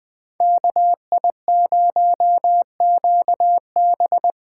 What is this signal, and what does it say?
Morse code, heard as a keyed tone, about 20 wpm, 700 Hz. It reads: KI0QB
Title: USA Call Signs - 3x Character Spacing - 20wpm